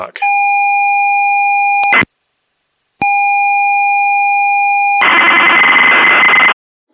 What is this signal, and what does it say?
ZX Spectrum tape loading.